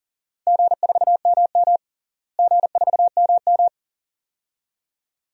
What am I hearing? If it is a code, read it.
Morse code, heard as a keyed tone, about 40 wpm, 700 Hz. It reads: G4MM G4MM
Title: DXCC-100 Call Signs - 4 Characters - Repeated 2x - 40wpm